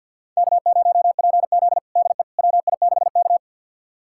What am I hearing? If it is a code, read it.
Morse code, heard as a keyed tone, about 52 wpm, 700 Hz. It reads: K0PZ DE WI6K
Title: USA Callsign DE Callsign - 4 Characters - 50wpm